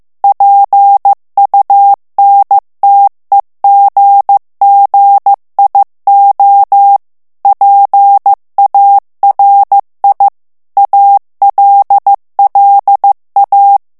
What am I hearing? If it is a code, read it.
Morse code, heard as a keyed tone, about 15 wpm, 800 Hz. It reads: PUNTEGGIO PARI ALLA
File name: morse mp3 / ITALIEN texte clair 2 vitesse 14 mots minute